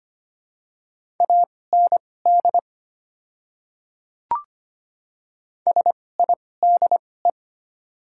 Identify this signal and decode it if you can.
Morse code, heard as a keyed tone, about 25 wpm, spaced as for 18 wpm, 700 Hz. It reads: AND SIDE
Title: 2x Character Spacing - Top 100 Words - 25wpm